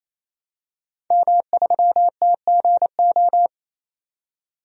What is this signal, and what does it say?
Morse code, heard as a keyed tone, about 28 wpm, 700 Hz. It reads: M3TGO
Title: DXCC-100 Call Signs - 5 Characters - 28wpm